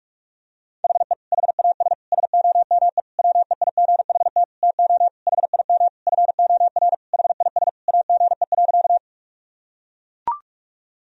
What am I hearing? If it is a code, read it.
Morse code, heard as a keyed tone, about 42 wpm, 700 Hz. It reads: HE HAS SOME WEIGHT TO HIM FOR HIS AGE.
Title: Sentences from Top 1000 Words - Review Part 3 - 45wpm